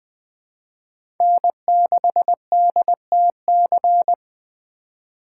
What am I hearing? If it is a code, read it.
Morse code, heard as a keyed tone, about 21 wpm, 700 Hz. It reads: N6DTC
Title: USA Call Signs - 5 Characters - 3x Character Spacing - 20wpm